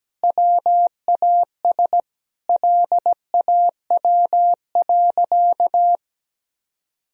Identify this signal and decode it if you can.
Morse code, heard as a keyed tone, about 17 wpm, 700 Hz. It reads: WAS LAW.
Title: Sentences from Top 1000 Words - Review Part 3 - 17wpm